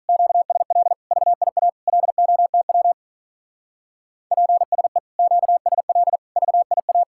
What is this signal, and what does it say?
Morse code, heard as a keyed tone, about 45 wpm, 700 Hz. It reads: QSL VIA LOTW PSE QSL VIA